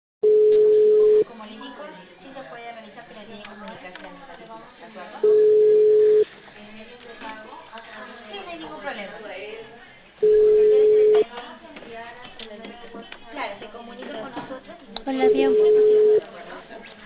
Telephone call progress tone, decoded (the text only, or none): ringback tone